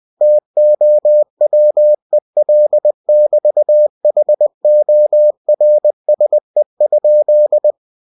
Morse code, heard as a keyed tone, about 20 wpm, 600 Hz. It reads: TOWEL<BT>HORSE?
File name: cw-books / winnie-the-pooh / winnie-20wpm-page-0019